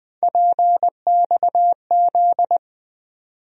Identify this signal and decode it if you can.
Morse code, heard as a keyed tone, about 20 wpm, 700 Hz. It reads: PXZ
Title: USA Call Sign Postfixes - 3 Letters 20wpm